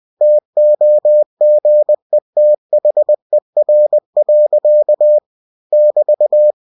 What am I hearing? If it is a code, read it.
Morse code, heard as a keyed tone, about 20 wpm, 600 Hz. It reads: TOGETHER. <BT>
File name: cw-books / winnie-the-pooh / winnie-20wpm-page-0038